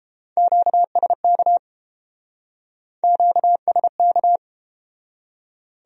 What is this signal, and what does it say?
Morse code, heard as a keyed tone, about 31 wpm, 700 Hz. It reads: QSK QSK